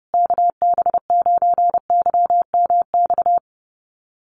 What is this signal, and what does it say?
Morse code, heard as a keyed tone, about 30 wpm, 700 Hz. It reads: KB9YMX
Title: USA Call Signs Repeated 3x 30wpm